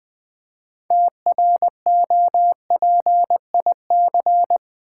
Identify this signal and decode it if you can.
Morse code, heard as a keyed tone, about 20 wpm, 700 Hz. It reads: TROPIC
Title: Build A Word - 2 Letters At A Time - 8-Letter Words - 20wpm